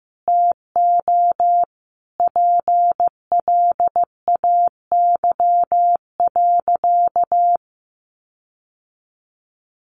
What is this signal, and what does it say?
Morse code, heard as a keyed tone, about 15 wpm, 700 Hz. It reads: TO PLAY.
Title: Sentences from Top 1500 Words 15wpm